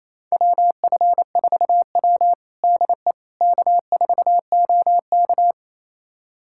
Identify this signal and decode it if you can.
Morse code, heard as a keyed tone, about 28 wpm, 700 Hz. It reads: WF4W DE K4OK
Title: USA Callsign DE Callsign - 4 Characters - 28wpm